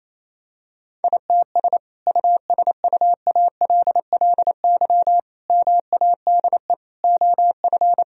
Morse code, heard as a keyed tone, about 27 wpm, 700 Hz. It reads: ITS USUALLY MADE OF
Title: Sentences from Top 1000 Words - Review Part 3 - 28wpm